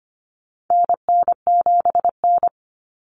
Morse code, heard as a keyed tone, about 25 wpm, 700 Hz. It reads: NN7N
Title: Common Cut Numbers - 4-Digit 25wpm